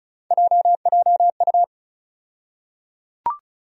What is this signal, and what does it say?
Morse code, heard as a keyed tone, about 36 wpm, 700 Hz. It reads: JJU
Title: USA Call Sign Postfixes - 3 Letters 35wpm